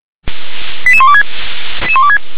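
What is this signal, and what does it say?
Five-tone selective call (ZVEI), heard as quick runs of five tones, tones of about 70 ms. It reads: 88126
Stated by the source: ZVEI five-tone selective call received on SDR.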